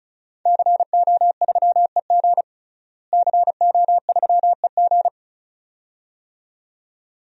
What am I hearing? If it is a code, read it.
Morse code, heard as a keyed tone, about 36 wpm, 700 Hz. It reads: CO3EG CO3EG
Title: DXCC-100 Call Signs - 5 Characters - Repeated 2x - 35wpm